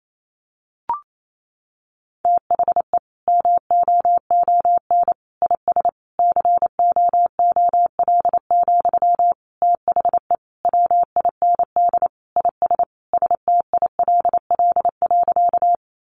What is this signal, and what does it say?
Morse code, heard as a keyed tone, about 28 wpm, 700 Hz. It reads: THE MOON IS COOL, THE WIND IS STILL.
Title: Sentences from Top 1200 Words 28wpm